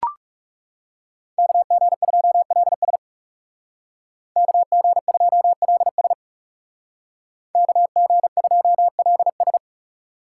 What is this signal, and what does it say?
Morse code, heard as a keyed tone, about 39 wpm, 700 Hz. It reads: KG2LS KG2LS KG2LS